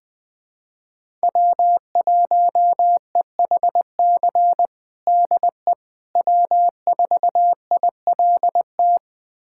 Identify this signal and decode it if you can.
Morse code, heard as a keyed tone, about 20 wpm, 700 Hz. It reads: W1EHC DE W4ILT
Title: USA Callsign DE Callsign - 5 Characters - 20wpm